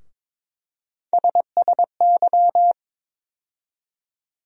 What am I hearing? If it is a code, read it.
Morse code, heard as a keyed tone, about 21 wpm, 700 Hz. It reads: SSY
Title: USA Call Sign Postfixes - 3 Letters 22wpm